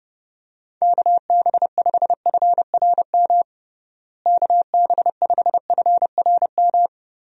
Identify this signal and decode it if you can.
Morse code, heard as a keyed tone, about 32 wpm, 700 Hz. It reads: KB5FRM KB5FRM
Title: USA Call Signs - 6 Characters - Repeated 2x - 30wpm